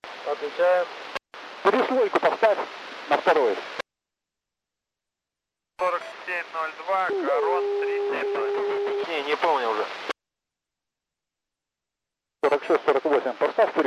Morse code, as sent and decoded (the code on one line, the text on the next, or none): none